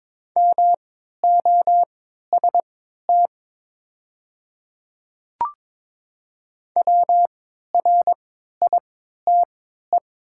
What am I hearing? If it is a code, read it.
Morse code, heard as a keyed tone, about 22 wpm, spaced as for 12 wpm, 700 Hz. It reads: MOST WRITE
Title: 3x Character Spacing - Top 100 Words - 22wpm